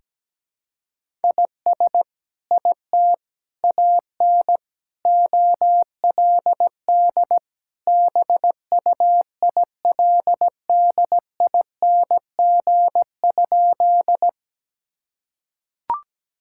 Morse code, heard as a keyed tone, about 17 wpm, 700 Hz. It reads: IS IT AN OLD BUILDING?
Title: Sentences from Top 1000 Words - Review Part 5 - 17wpm